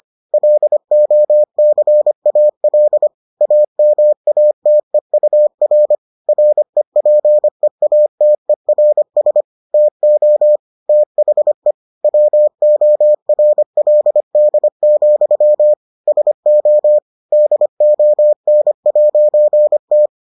Morse code, heard as a keyed tone, about 25 wpm, 600 Hz. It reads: LOCAL AMATEUR REPEATERS TO THE WORLD, SO DON'T